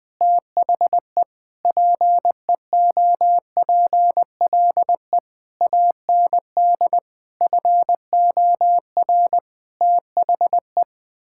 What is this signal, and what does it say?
Morse code, heard as a keyed tone, about 20 wpm, 700 Hz. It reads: THE PEOPLE AND FOR THE
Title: Sentences with 9-Word Phrases - Spoken Beginning - 20wpm